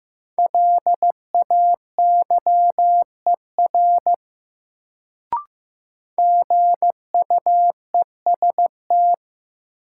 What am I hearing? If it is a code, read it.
Morse code, heard as a keyed tone, about 15 wpm, 700 Hz. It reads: LAYER GUEST
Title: Sentences from Top 1600 Words - New Word Drill - 15wpm